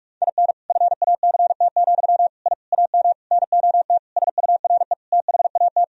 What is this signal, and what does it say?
Morse code, heard as a keyed tone, about 42 wpm, 700 Hz. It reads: IN FACT, I AM NOT SURE THAT
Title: Sentences from Top 1000 Words - Review Part 3 - 45wpm